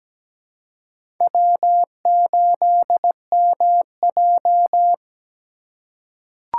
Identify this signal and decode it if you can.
Morse code, heard as a keyed tone, about 17 wpm, 700 Hz. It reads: W8MJ
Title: USA Call Signs - 4 Characters - 3x Character Spacing - 17wpm